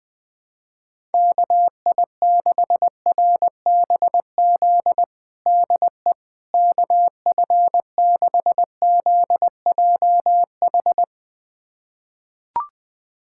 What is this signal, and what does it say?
Morse code, heard as a keyed tone, about 20 wpm, 700 Hz. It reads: KI6RBZ DE KF6ZJH
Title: USA Callsign DE Callsign - 6 Characters - 20wpm